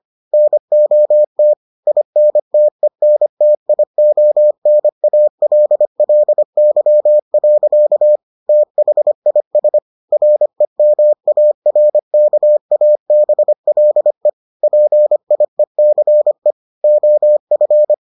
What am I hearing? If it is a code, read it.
Morse code, heard as a keyed tone, about 25 wpm, 600 Hz. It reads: NOT INTENTIONALLY. THIS REMARKABLE PIECE OF